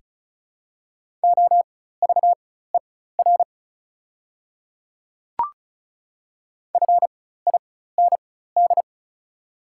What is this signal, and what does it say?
Morse code, heard as a keyed tone, about 36 wpm, spaced as for 16 wpm, 700 Hz. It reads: OVER FIND
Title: 4x Character Spacing - Top 100 Words - 35wpm